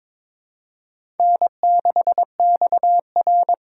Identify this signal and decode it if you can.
Morse code, heard as a keyed tone, about 22 wpm, 700 Hz. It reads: N6XR
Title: USA Call Signs - 4 Characters - 3x Character Spacing - 22wpm